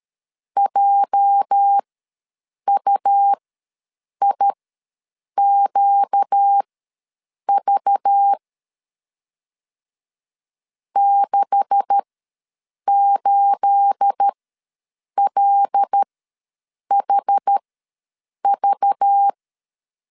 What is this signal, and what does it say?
Morse code, heard as a keyed tone, about 13 wpm, spaced as for 7 wpm, 800 Hz. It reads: JUIQV 68LHV
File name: LEZIONI / 9 286 ws 12wpm 3351